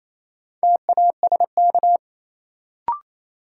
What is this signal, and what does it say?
Morse code, heard as a keyed tone, about 27 wpm, 700 Hz. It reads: TASK
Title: Sentences from Top 1000 Words - Review Part 7 - New Word Drill - 28wpm